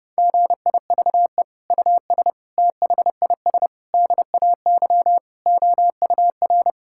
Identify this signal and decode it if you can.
Morse code, heard as a keyed tone, about 30 wpm, 700 Hz. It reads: GIVE US THIS DAY OUR
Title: Sentences from Top 1600 Words 30wpm